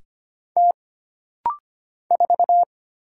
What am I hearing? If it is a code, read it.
Morse code, heard as a keyed tone, about 25 wpm, 700 Hz. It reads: T 4
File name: Single Character (Letter:Number:Punctuation:Prosign) - Rapid-Fire - 25wpm